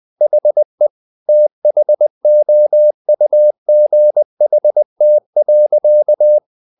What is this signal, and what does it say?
Morse code, heard as a keyed tone, about 20 wpm, 600 Hz. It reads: HE THOUGHT.
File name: the old man and the sea part 2 20wpm